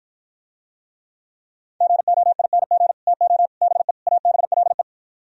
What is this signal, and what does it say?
Morse code, heard as a keyed tone, about 52 wpm, 700 Hz. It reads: GOING TO BE ABLE